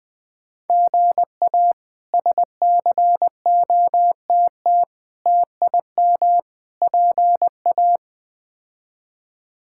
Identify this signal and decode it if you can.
Morse code, heard as a keyed tone, about 20 wpm, 700 Hz. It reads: GA SCOTT TIM PA
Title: K1USN SST QSO Practice - 20wpm